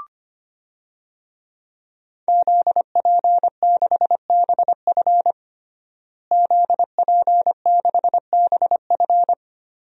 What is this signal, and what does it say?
Morse code, heard as a keyed tone, about 25 wpm, 700 Hz. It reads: ZP6BF ZP6BF
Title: DXCC-100 Call Signs - 5 Characters - Repeated 3x - 25wpm